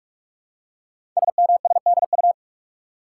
Morse code, heard as a keyed tone, about 44 wpm, 700 Hz. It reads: SMSDU
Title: Code Groups - 5 Letters - 45wpm